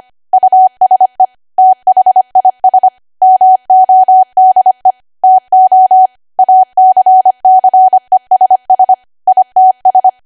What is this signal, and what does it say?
Morse code, heard as a keyed tone, about 25 wpm, 750 Hz. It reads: USE THIS MODE TO ACCESS ITS